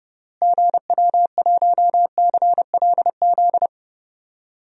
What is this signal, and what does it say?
Morse code, heard as a keyed tone, about 30 wpm, 700 Hz. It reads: GW1CLZ
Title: DXCC-100 Call Signs - 6 Characters - Repeated 3x - 30wpm